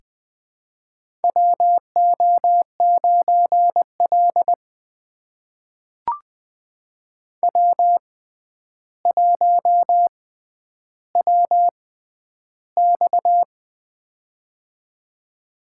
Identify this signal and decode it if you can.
Morse code, heard as a keyed tone, about 20 wpm, spaced as for 10 wpm, 700 Hz. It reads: WO9L W1WX
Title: USA Call Signs - 4 Characters - 6x Character Spacing - 20wpm